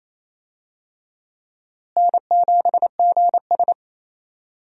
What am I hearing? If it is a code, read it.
Morse code, heard as a keyed tone, about 28 wpm, 700 Hz. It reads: N7GS
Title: USA Call Signs - 4 Characters - Repeated 2x - 28wpm